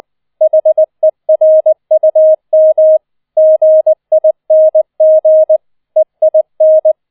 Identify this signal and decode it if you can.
Morse code, heard as a keyed tone, about 20 wpm, 630 Hz. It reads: HERUM GING EIN